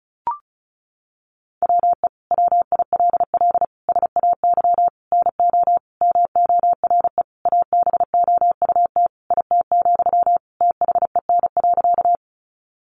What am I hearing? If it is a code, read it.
Morse code, heard as a keyed tone, about 34 wpm, 700 Hz. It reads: WE WILL SAY NO MORE ABOUT IT, THEN.
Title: Sentences from Top 200 Words 35wpm